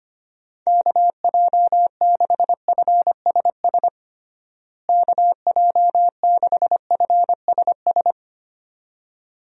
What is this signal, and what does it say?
Morse code, heard as a keyed tone, about 25 wpm, 700 Hz. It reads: KJ6FSS KJ6FSS
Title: USA Call Signs - 6 Characters - Repeated 2x - 25wpm